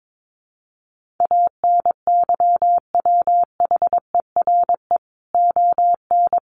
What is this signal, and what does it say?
Morse code, heard as a keyed tone, about 22 wpm, 700 Hz. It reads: ANYWHERE ON
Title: Sentences from Top 1400 Words 22wpm